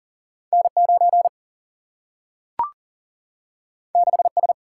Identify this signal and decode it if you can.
Morse code, heard as a keyed tone, about 38 wpm, 700 Hz. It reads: N 9 6 S
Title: Single-Letter Single-Number 40wpm